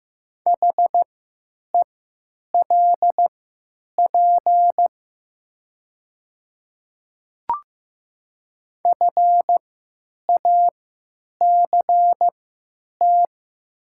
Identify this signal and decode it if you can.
Morse code, heard as a keyed tone, about 15 wpm, spaced as for 9 wpm, 700 Hz. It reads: HELP FACT
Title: 3x Character Spacing - Top 500 Words - 15wpm